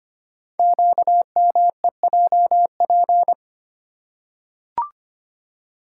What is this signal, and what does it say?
Morse code, heard as a keyed tone, about 25 wpm, 700 Hz. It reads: QMEJP
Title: Code Groups - 5 Letter:Numbers - 25wpm